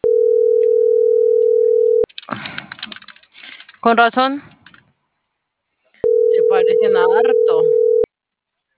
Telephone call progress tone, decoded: ringback tone

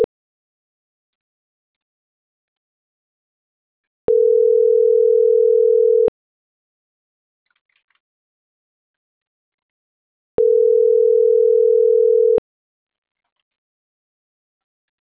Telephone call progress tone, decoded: ringback tone

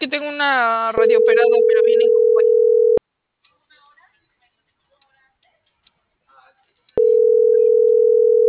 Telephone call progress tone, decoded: ringback tone

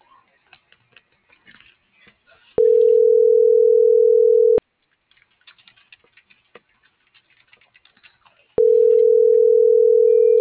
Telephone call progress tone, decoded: ringback tone

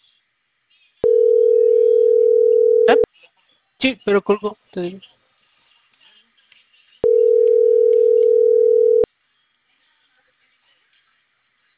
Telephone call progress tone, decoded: ringback tone

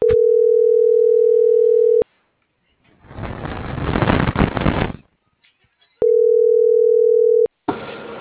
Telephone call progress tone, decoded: ringback tone

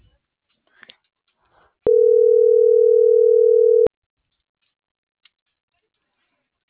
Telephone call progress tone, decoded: ringback tone

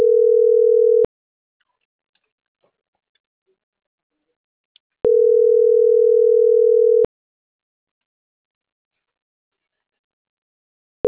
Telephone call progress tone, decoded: ringback tone

